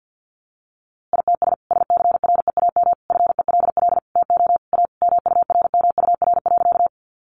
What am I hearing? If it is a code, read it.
Morse code, heard as a keyed tone, about 50 wpm, 700 Hz. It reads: ITS SCREAM FELL TO A MURMUR.
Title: A Sound Of Thunder 50wpm